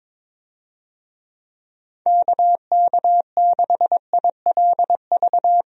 Morse code, heard as a keyed tone, about 22 wpm, 700 Hz. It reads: KK6ILV
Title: USA Call Signs - 6 Characters - Repeated 2x - 22wpm